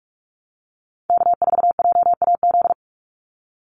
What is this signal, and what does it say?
Morse code, heard as a keyed tone, about 44 wpm, 700 Hz. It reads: K4JAZ
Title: Call Signs Repeated 3x - 45wpm